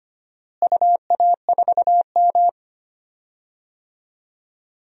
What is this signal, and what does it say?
Morse code, heard as a keyed tone, about 25 wpm, 700 Hz. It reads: UA4M
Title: DXCC-100 Call Signs - 4 Characters - Repeated 2x - 25wpm